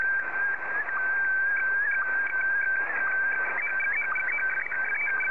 SSTV (slow-scan television), Scottie DX.
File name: sstv / SDX